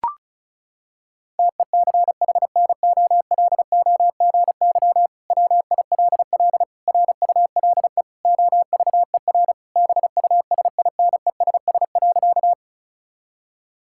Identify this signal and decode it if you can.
Morse code, heard as a keyed tone, about 36 wpm, 700 Hz. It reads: TECHNOLOGY WILL RULE OVER BUSINESS.
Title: Sentences from Top 1000 Words - Review Part 5 - 35wpm